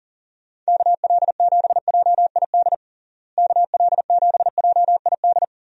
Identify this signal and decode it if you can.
Morse code, heard as a keyed tone, about 40 wpm, 700 Hz. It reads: KL7JID KL7JID
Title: DXCC-100 Call Signs - 6 Characters - Repeated 2x - 40wpm